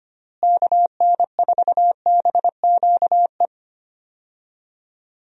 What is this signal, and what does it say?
Morse code, heard as a keyed tone, about 25 wpm, 700 Hz. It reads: KN4BQE
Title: USA Call Signs - 6 Characters - Repeated 2x - 25wpm